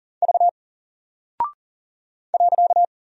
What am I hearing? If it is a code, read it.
Morse code, heard as a keyed tone, about 40 wpm, 700 Hz. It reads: V .
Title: Single Character (Letter:Number:Punctuation:Prosign) Alt 40wpm